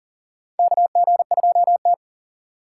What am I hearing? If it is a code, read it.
Morse code, heard as a keyed tone, about 39 wpm, 690 Hz. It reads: KG2T
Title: USA Call Signs - 4 Characters - 2x Character Spacing - 40wpm